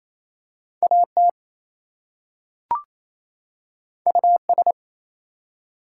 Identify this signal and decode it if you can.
Morse code, heard as a keyed tone, about 28 wpm, 700 Hz. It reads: AT US
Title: Two Letter Words - 28wpm